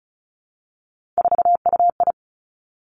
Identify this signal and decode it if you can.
Morse code, heard as a keyed tone, about 36 wpm, 700 Hz. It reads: 4UI
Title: Three Challenging Letter-Number Combinations 35wpm